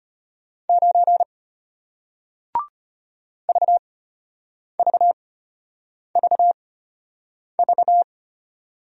Morse code, heard as a keyed tone, about 33 wpm, spaced as for 8 wpm, 700 Hz. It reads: 9 VVVV